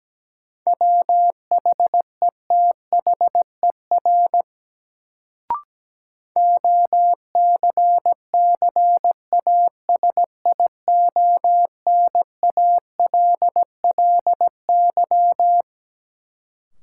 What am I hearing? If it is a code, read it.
Morse code, heard as a keyed tone, about 17 wpm, 700 Hz. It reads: WHETHER OCCASIONALLY
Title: Sentences from Top 1300 Words - New Word Drill Alt - 17wpm